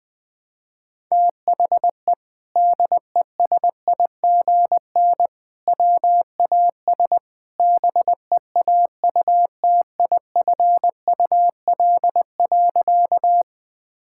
Morse code, heard as a keyed tone, about 20 wpm, 700 Hz. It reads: THE DESIGN WAS BEAUTIFUL.
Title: Sentences from Top 1000 Words - Review Part 5 - 20wpm